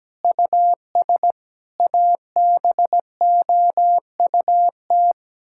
Morse code, heard as a keyed tone, about 17 wpm, 700 Hz. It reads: US ABOUT
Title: Sentences from Top 1000 Words - Review Part 6 - 17wpm